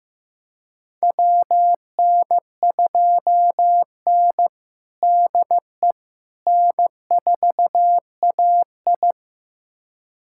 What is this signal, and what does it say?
Morse code, heard as a keyed tone, about 15 wpm, 700 Hz. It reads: WN2N DE N4AI
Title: USA Callsign DE Callsign - 4 Characters - 15wpm